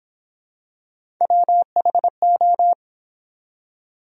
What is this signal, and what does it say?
Morse code, heard as a keyed tone, about 26 wpm, 700 Hz. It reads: WHO